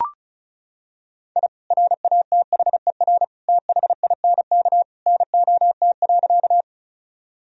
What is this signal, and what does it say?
Morse code, heard as a keyed tone, about 34 wpm, 700 Hz. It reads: I RATHER THINK NOT.
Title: Sentences from Top 1000 Words - Review Part 4 - 35wpm